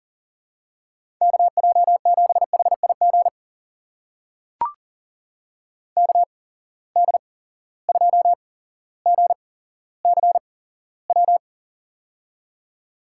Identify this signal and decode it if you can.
Morse code, heard as a keyed tone, about 39 wpm, spaced as for 17 wpm, 700 Hz. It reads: KJ7HIG KD2GCW
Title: USA Call Signs - 6 Characters - 8x Character Spacing - 40wpm